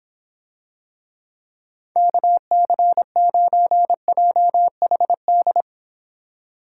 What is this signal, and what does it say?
Morse code, heard as a keyed tone, about 26 wpm, 700 Hz. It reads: KC9JHD